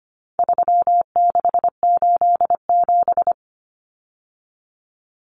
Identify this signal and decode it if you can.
Morse code, heard as a keyed tone, about 25 wpm, 700 Hz. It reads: 3687
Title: Numbers - 4-Digit 25wpm